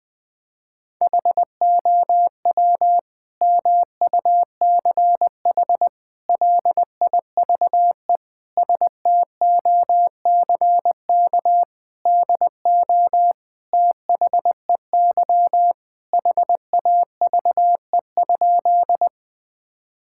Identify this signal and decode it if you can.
Morse code, heard as a keyed tone, about 20 wpm, 700 Hz. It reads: HOW MUCH LIVE STOCK DO THEY HAVE?
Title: Sentences from Top 1000 Words - Review Part 5 - 20wpm